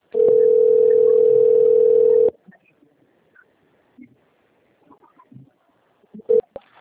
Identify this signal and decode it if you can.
Ringback tone (North America), 440 + 480 Hz.